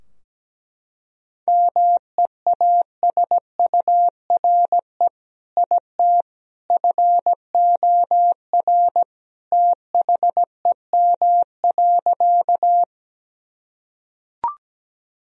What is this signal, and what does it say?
Morse code, heard as a keyed tone, about 17 wpm, 700 Hz. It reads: MEASURE IT FOR THEM.
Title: Sentences from Top 1000 Words - Review Part 2 - 17wpm